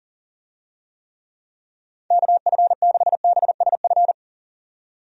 Morse code, heard as a keyed tone, about 40 wpm, 700 Hz. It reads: KF6BSF
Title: USA Call Signs - 6 Characters - 40wpm